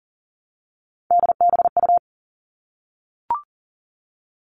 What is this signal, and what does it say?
Morse code, heard as a keyed tone, about 40 wpm, 700 Hz. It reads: DBU
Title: All Cut Numbers - 3-Digit 40wpm